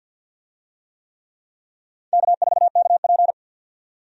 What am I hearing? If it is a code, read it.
Morse code, heard as a keyed tone, about 52 wpm, 700 Hz. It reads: K4KP
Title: USA Call Signs - 4 Characters - 50wpm